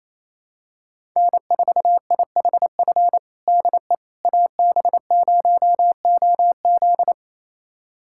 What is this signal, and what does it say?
Morse code, heard as a keyed tone, about 28 wpm, 700 Hz. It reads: N4IHF DE AB0OZ
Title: USA Callsign DE Callsign - 5 Characters - 28wpm